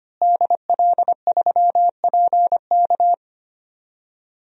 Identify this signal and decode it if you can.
Morse code, heard as a keyed tone, about 25 wpm, 700 Hz. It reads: DL3PK
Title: DXCC-100 Call Signs - 5 Characters - Repeated 2x - 25wpm